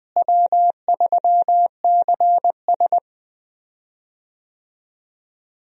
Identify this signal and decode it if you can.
Morse code, heard as a keyed tone, about 20 wpm, 700 Hz. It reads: W3CS
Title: USA Call Signs - 4 Characters - Repeated 2x - 20wpm